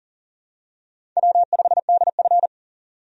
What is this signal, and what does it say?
Morse code, heard as a keyed tone, about 40 wpm, 700 Hz. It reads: W5DF
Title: USA Call Signs - 4 Characters - 40wpm